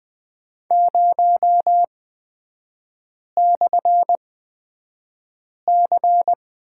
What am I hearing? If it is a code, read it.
Morse code, heard as a keyed tone, about 20 wpm, spaced as for 5 wpm, 700 Hz. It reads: 0/C
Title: 243 Lesson 39e - Letter z - Three Characters 20wpm